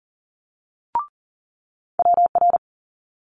Morse code, heard as a keyed tone, about 41 wpm, 700 Hz. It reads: WR
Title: Multi-letter Phonemes Quick Fire 40wpm